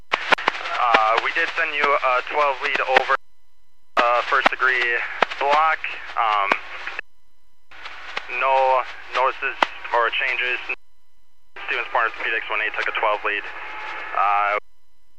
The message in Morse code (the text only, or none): none